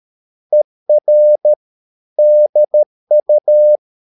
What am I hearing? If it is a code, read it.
Morse code, heard as a keyed tone, about 13 wpm, 600 Hz. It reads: ER DU
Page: Ferie 13 wpm